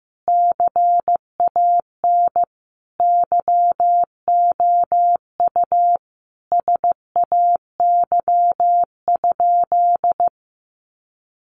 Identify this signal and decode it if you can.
Morse code, heard as a keyed tone, about 15 wpm, 700 Hz. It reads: CAN YOU SAY?
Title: Sentences from Top 800 Words 15wpm